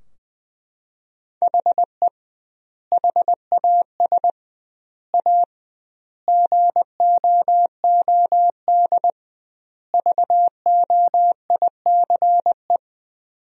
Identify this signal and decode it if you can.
Morse code, heard as a keyed tone, about 21 wpm, 700 Hz. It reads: HE HAS A GOOD VOICE
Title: 2x Word-Spacing - Sentences from Top 500 Words - 20wpm